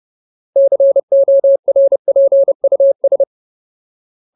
Morse code, heard as a keyed tone, about 30 wpm, 550 Hz. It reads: CORPUS
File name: Ultimate Word List / 30WPM Ultimate